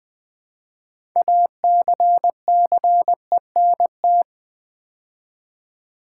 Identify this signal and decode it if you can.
Morse code, heard as a keyed tone, about 20 wpm, 690 Hz. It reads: ACCENT
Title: Build A Word - 2 Letters At A Time - 6-Letter Words - 20wpm